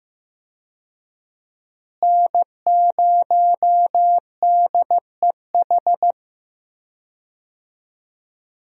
Morse code, heard as a keyed tone, about 15 wpm, 700 Hz. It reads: N0DEH
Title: USA Call Signs - 5 Characters - Repeated 3x - 15wpm